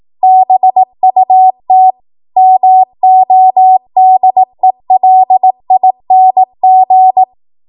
Morse code, heard as a keyed tone, about 18 wpm, 750 Hz. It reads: BUT MODELING
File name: Morse / 18WPM